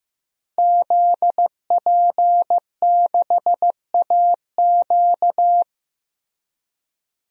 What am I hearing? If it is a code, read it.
Morse code, heard as a keyed tone, about 15 wpm, 700 Hz. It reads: ZP6AQ
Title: DXCC-100 Call Signs - 5 Characters - Repeated 2x - 15wpm